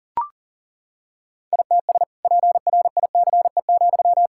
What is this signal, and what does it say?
Morse code, heard as a keyed tone, about 40 wpm, 700 Hz. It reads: ITS PRICE,
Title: Sentences from Top 1000 Words - Review Part 5 - 40wpm